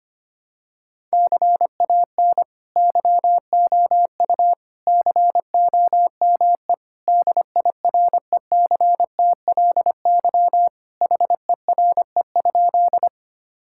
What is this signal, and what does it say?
Morse code, heard as a keyed tone, about 25 wpm, 700 Hz. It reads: CAN YOU COME DIRECTLY HERE?
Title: Sentences from Top 1000 Words - Review Part 5 - 25wpm